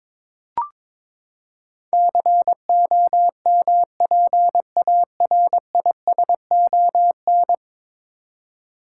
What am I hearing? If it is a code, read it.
Morse code, heard as a keyed tone, about 22 wpm, 700 Hz. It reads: COMPARISON
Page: morse-code-ninja-practice-22wpm